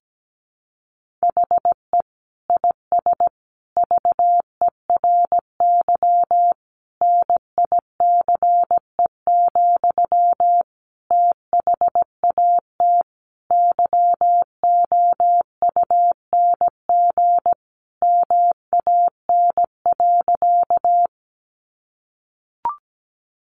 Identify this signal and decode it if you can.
Morse code, heard as a keyed tone, about 17 wpm, 700 Hz. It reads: HE IS VERY NICE, THAT YOUNG MAN.
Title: Sentences from Top 1100 Words 17wpm